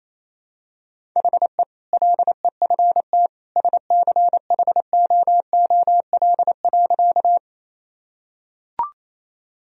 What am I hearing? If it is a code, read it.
Morse code, heard as a keyed tone, about 27 wpm, 700 Hz. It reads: HE LEFT SCHOOL.
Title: Sentences from Top 1000 Words - Review Part 1 - 28wpm